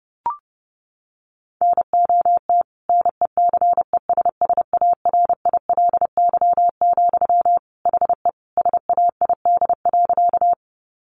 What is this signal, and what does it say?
Morse code, heard as a keyed tone, about 28 wpm, 700 Hz. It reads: NOT NECESSARILY, HE SAID.
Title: Sentences from Top 1400 Words 30wpm